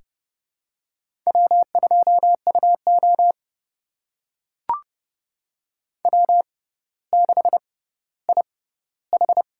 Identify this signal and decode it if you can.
Morse code, heard as a keyed tone, about 29 wpm, spaced as for 15 wpm, 700 Hz. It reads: W2UO W6IH
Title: USA Call Signs - 4 Characters - 6x Character Spacing - 30wpm